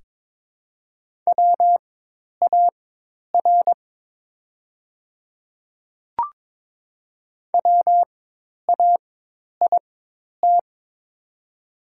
Morse code, heard as a keyed tone, about 22 wpm, spaced as for 10 wpm, 700 Hz. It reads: WAR WAIT
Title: 4x Character Spacing - Top 500 Words - 22wpm